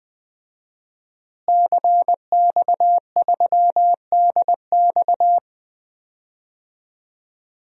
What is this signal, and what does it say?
Morse code, heard as a keyed tone, about 21 wpm, 700 Hz. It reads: CX3DX
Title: DXCC-100 Call Signs - 5 Characters - Repeated 3x - 20wpm